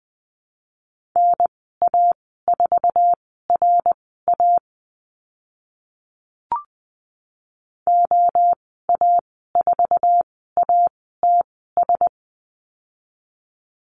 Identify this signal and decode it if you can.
Morse code, heard as a keyed tone, about 20 wpm, spaced as for 14 wpm, 700 Hz. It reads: NA4RA OA4ATS
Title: 076 Lesson 13g - Letter L - Call Signs 20wpm